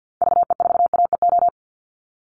Morse code, heard as a keyed tone, about 48 wpm, 700 Hz. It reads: VE4AEO
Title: Call Signs Repeated 3x - 50wpm